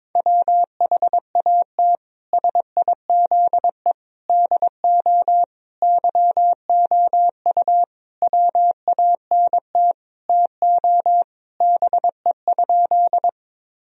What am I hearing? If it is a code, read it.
Morse code, heard as a keyed tone, about 22 wpm, 700 Hz. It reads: WHAT SIZE DO YOU WANT TO BE?
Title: Sentences from Top 1000 Words - Review Part 2 - 22wpm